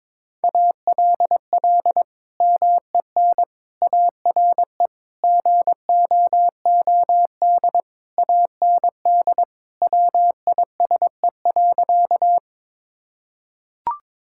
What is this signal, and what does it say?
Morse code, heard as a keyed tone, about 22 wpm, 700 Hz. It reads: ALL MEN ARE GOOD AND WISE.
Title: Sentences from Top 1600 Words 22wpm